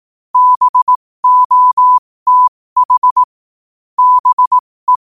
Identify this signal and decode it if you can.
Morse code, heard as a keyed tone, about 18 wpm, spaced as for 15 wpm, 1000 Hz. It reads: BOTH BE
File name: qotdfiles / 15wpm